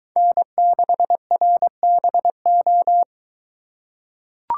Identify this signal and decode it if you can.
Morse code, heard as a keyed tone, about 23 wpm, 700 Hz. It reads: N6RBO